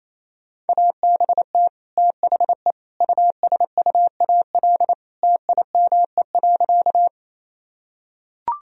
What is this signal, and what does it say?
Morse code, heard as a keyed tone, about 28 wpm, 700 Hz. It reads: ABT THE USUAL TIME.
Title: Sentences from Top 400 Words and Abbreviations 28wpm